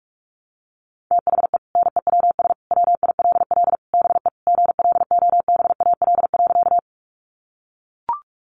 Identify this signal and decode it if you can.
Morse code, heard as a keyed tone, about 44 wpm, 700 Hz. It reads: THE NEWS WILL BE GLOBAL.
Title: Sentences from Top 1200 Words 45wpm